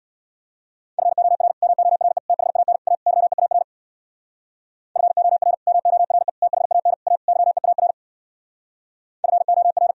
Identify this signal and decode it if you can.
Morse code, heard as a keyed tone, about 65 wpm, 700 Hz. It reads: FOR MORE THAN A YEAR FOR MORE THAN A YEAR FOR